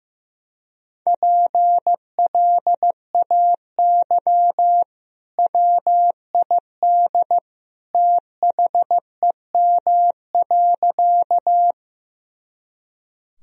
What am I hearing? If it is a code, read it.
Morse code, heard as a keyed tone, about 15 wpm, 700 Hz. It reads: PLAY WID THEM.
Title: Sentences from Top 200 Words and Abbreviations 15wpm